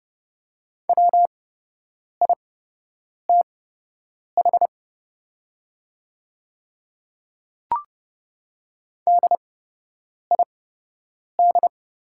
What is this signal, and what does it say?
Morse code, heard as a keyed tone, about 30 wpm, spaced as for 8 wpm, 700 Hz. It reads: WITH DID
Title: 8x Character Spacing - Top 100 Words - 30wpm